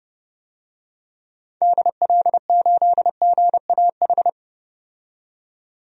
Morse code, heard as a keyed tone, about 30 wpm, 700 Hz. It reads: DL8GAH
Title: DXCC-100 Call Signs - 6 Characters - 30wpm